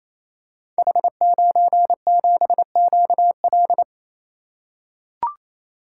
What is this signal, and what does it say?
Morse code, heard as a keyed tone, about 27 wpm, 700 Hz. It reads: H97QL
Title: Code Groups - 5 Letter:Numbers - 28wpm